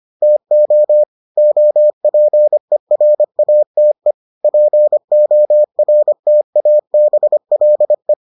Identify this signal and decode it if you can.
Morse code, heard as a keyed tone, about 25 wpm, 600 Hz. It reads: TO OPERATE PORTABLE